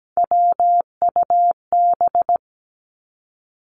Morse code, heard as a keyed tone, about 17 wpm, 700 Hz. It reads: WUB
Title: Three Challenging Letter-Number Combinations 17wpm